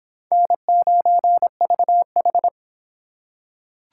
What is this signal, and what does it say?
Morse code, heard as a keyed tone, about 25 wpm, 700 Hz. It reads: N9VH